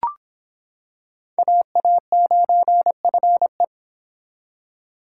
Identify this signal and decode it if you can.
Morse code, heard as a keyed tone, about 26 wpm, 700 Hz. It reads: AA9FE